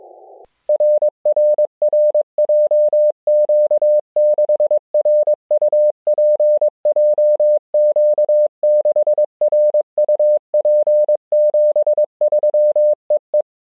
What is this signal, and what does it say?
Morse code, heard as a keyed tone, about 22 wpm, 600 Hz. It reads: RRR JQ6RUP JQ6RUP 73 EE